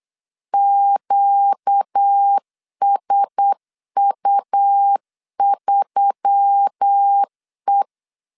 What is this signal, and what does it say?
Morse code, heard as a keyed tone, about 8 wpm, 800 Hz. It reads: QSU3E
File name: LEZIONI / 6 PKX0 / 6 PKX 8wpm 3041